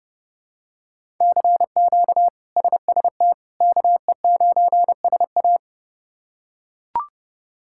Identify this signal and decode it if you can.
Morse code, heard as a keyed tone, about 30 wpm, 700 Hz. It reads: CQ SST KE9SA
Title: K1USN SST QSO Practice - 30wpm